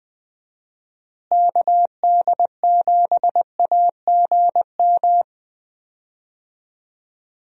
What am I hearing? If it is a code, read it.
Morse code, heard as a keyed tone, about 21 wpm, 700 Hz. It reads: KD7AGM
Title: USA Call Signs - 6 Characters - Repeated 2x - 20wpm